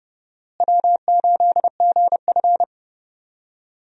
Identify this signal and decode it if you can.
Morse code, heard as a keyed tone, about 30 wpm, 700 Hz. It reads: W8GF
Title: USA Call Signs - 4 Characters - Repeated 2x - 30wpm